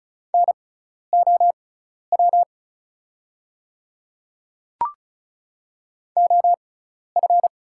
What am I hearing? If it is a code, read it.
Morse code, heard as a keyed tone, about 34 wpm, spaced as for 12 wpm, 690 Hz. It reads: NOW OF
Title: 6x Character Spacing - Top 100 Words - 35wpm